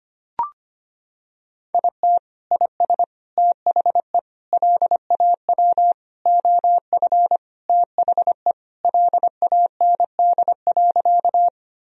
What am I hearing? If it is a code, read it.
Morse code, heard as a keyed tone, about 25 wpm, 700 Hz. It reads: IT IS THE LAW OF THE LAND.
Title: Sentences from Top 1000 Words - Review Part 3 - 25wpm